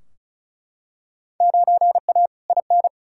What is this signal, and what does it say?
Morse code, heard as a keyed tone, about 36 wpm, 700 Hz. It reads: 9A IN
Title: Field Day Exchanges Encore - 35wpm